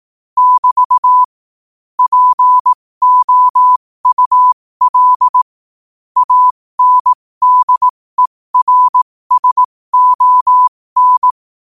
Morse code, heard as a keyed tone, about 18 wpm, spaced as for 15 wpm, 1000 Hz. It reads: <BT> POUL ANDERSON
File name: qotdfiles / 15wpm